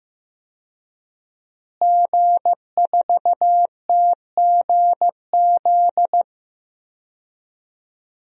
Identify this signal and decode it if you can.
Morse code, heard as a keyed tone, about 15 wpm, 700 Hz. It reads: G4TGZ
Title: DXCC-100 Call Signs - 5 Characters - Repeated 2x - 15wpm